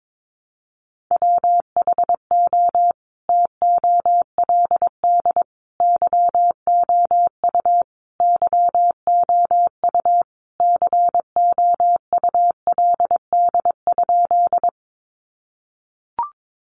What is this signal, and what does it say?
Morse code, heard as a keyed tone, about 22 wpm, 700 Hz. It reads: WHO TOLD YOU YOU COULD?
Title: Sentences from Top 400 Words 22wpm